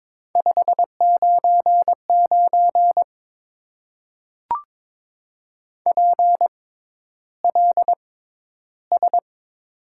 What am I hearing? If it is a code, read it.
Morse code, heard as a keyed tone, about 22 wpm, spaced as for 11 wpm, 700 Hz. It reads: 599 PLS
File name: Top 90 QSO Elements - 6x Character Spacing - 22wpm